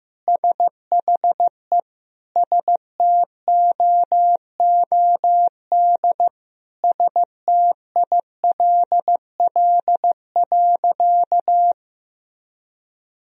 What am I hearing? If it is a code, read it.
Morse code, heard as a keyed tone, about 15 wpm, 700 Hz. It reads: SHE STOOD STILL.
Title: Sentences from Top 1000 Words - Review Part 2 - 15wpm